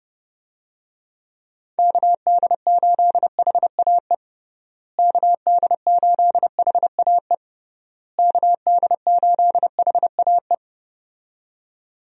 Morse code, heard as a keyed tone, about 32 wpm, 700 Hz. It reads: KD8HAE KD8HAE KD8HAE
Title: USA Call Signs - 6 Characters - Repeated 3x - 30wpm